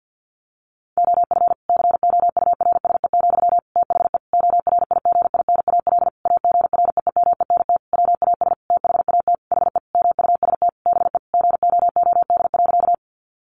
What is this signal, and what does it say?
Morse code, heard as a keyed tone, about 50 wpm, 700 Hz. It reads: OF COURSE, THE ORIGINAL AGREEMENT WAS THAT HE MUST BE GOOD.
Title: Sentences from Top 1500 Words 50wpm